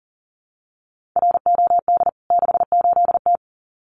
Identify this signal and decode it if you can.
Morse code, heard as a keyed tone, about 37 wpm, 700 Hz. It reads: ROD 68T
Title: CWT Exchange 40wpm